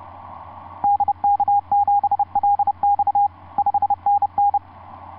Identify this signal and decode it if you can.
Morse code, heard as a keyed tone, about 30 wpm, 810 Hz, with radio noise. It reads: DK7LX 5NN
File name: s79gm on4on etc 15m cw 30mar2010 0653